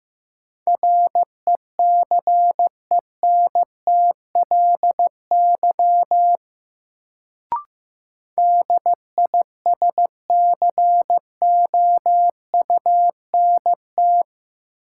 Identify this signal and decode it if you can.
Morse code, heard as a keyed tone, about 15 wpm, 700 Hz. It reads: RECENTLY DISCOUNT
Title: Sentences from Top 1200 Words - New Word Drill - 15wpm